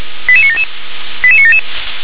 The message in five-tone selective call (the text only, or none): 88977 88977